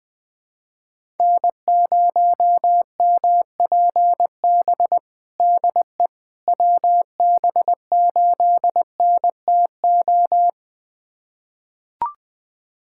Morse code, heard as a keyed tone, about 21 wpm, 700 Hz. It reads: N0MPB DE WB8NTO
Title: USA Callsign DE Callsign 20wpm